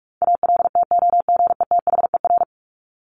Morse code, heard as a keyed tone, about 44 wpm, 700 Hz. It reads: ALTOGETHER
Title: Top 1501-1750 Words 45wpm